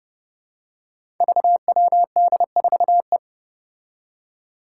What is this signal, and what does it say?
Morse code, heard as a keyed tone, about 29 wpm, 700 Hz. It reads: VWD4E
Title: Five Challenging Letter-Number Combinations 30wpm